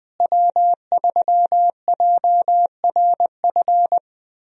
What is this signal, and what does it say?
Morse code, heard as a keyed tone, about 20 wpm, 700 Hz. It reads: W3JRF
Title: USA Call Signs - 5 Characters - Repeated 2x - 20wpm